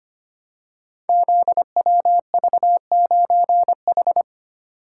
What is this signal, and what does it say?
Morse code, heard as a keyed tone, about 25 wpm, 690 Hz. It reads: ZWV9H
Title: Code Groups - 5 Letter:Numbers - 25wpm